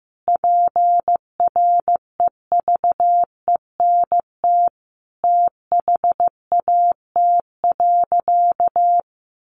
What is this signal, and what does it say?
Morse code, heard as a keyed tone, about 15 wpm, 700 Hz. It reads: PREVENT THAT.
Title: Sentences from Top 1100 Words 15wpm